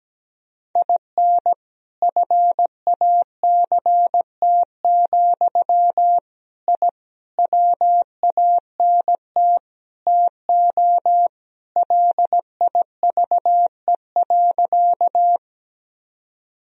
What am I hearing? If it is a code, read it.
Morse code, heard as a keyed tone, about 17 wpm, 700 Hz. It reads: IN FACT, I WANT TO LIVE.
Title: Sentences from Top 1000 Words - Review Part 3 - 17wpm